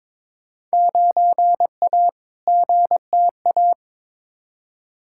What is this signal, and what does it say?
Morse code, heard as a keyed tone, about 22 wpm, 700 Hz. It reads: 9A GTA
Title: Field Day Exchanges Encore - 22wpm